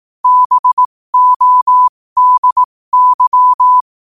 Morse code, heard as a keyed tone, about 18 wpm, spaced as for 15 wpm, 1000 Hz. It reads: BODY
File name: qotdfiles / 15wpm